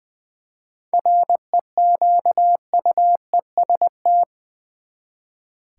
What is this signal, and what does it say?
Morse code, heard as a keyed tone, about 20 wpm, 700 Hz. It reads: REQUEST
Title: Seven Letter Words - 20wpm